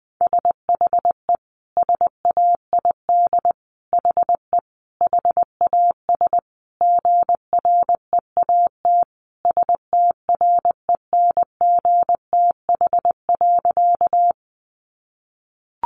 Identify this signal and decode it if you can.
Morse code, heard as a keyed tone, about 20 wpm, 700 Hz. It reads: SHE SAID HE HAS GREAT STRENGTH.
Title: Sentences from Top 1300 Words 20wpm